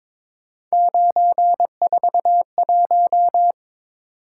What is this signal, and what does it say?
Morse code, heard as a keyed tone, about 26 wpm, 700 Hz. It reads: 941
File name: Speed-Racing - Three Letter-Number Combinations - 22wpm